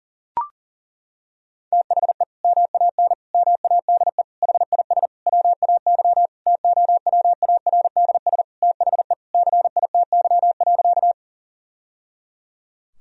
Morse code, heard as a keyed tone, about 40 wpm, 700 Hz. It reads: THE MAN MADE HIS WAY TOWARDS THE CITY.
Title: Sentences from Top 1000 Words - Review Part 2 - 40wpm